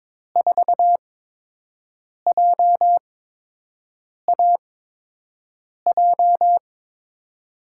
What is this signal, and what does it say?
Morse code, heard as a keyed tone, about 22 wpm, spaced as for 6 wpm, 700 Hz. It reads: 4JAJ
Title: USA Call Signs - 6 Characters - 8x Character Spacing - 22wpm